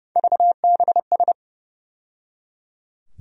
Morse code, heard as a keyed tone, about 30 wpm, 700 Hz. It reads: VBS
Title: USA Call Sign Postfixes - 3 Letters 30wpm